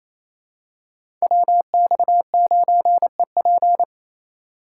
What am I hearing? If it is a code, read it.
Morse code, heard as a keyed tone, about 28 wpm, 700 Hz. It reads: WX9EP
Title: USA Call Signs - 5 Characters - 2x Character Spacing - 28wpm